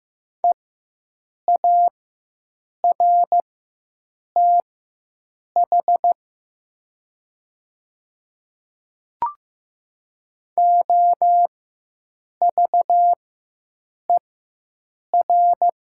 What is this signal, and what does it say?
Morse code, heard as a keyed tone, about 15 wpm, spaced as for 7 wpm, 700 Hz. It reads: EARTH OVER
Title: 4x Character Spacing - Top 500 Words - 15wpm